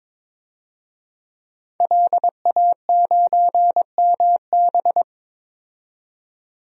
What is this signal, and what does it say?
Morse code, heard as a keyed tone, about 22 wpm, 700 Hz. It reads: LA9MB
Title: DXCC-100 Call Signs - 5 Characters - 22wpm